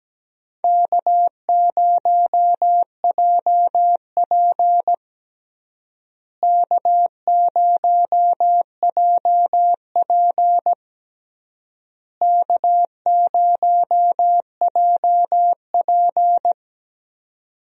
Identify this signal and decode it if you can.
Morse code, heard as a keyed tone, about 17 wpm, 700 Hz. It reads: K0JP K0JP K0JP
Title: USA Call Signs - 4 Characters - Repeated 3x - 17wpm